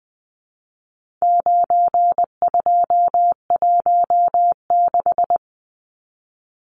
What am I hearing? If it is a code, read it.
Morse code, heard as a keyed tone, about 20 wpm, 700 Hz. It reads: 9216
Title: Numbers - 4-Digit 20wpm